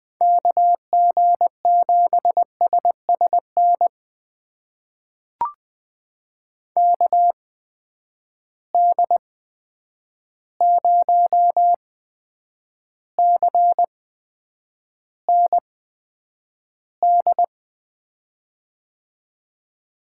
Morse code, heard as a keyed tone, about 20 wpm, spaced as for 9 wpm, 700 Hz. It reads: KG7SSN KD0CND
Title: USA Call Signs - 6 Characters - 8x Character Spacing - 20wpm